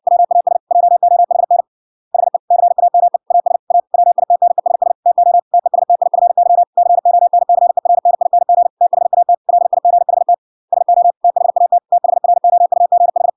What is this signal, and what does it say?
Morse code, heard as a keyed tone, about 63 wpm, 700 Hz. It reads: WAS JOHN HE CAME AS A WITNESS TO TESTIFY CONCERNING THAT LIGHT SO THAT THROUGH